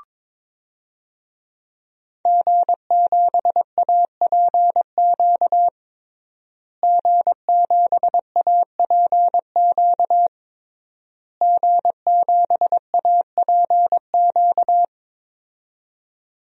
Morse code, heard as a keyed tone, about 22 wpm, 700 Hz. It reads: G7APQ G7APQ G7APQ
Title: DXCC-100 Call Signs - 5 Characters - Repeated 3x - 22wpm